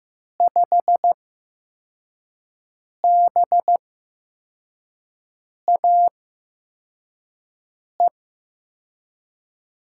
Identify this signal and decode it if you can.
Morse code, heard as a keyed tone, about 15 wpm, spaced as for 4 wpm, 700 Hz. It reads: 5BAE
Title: USA Call Signs - 6 Characters - 8x Character Spacing - 15wpm